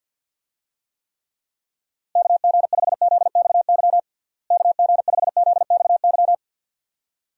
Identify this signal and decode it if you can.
Morse code, heard as a keyed tone, about 51 wpm, 700 Hz. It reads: KG5ZXY KG5ZXY
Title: USA Call Signs - 6 Characters - Repeated 2x - 50wpm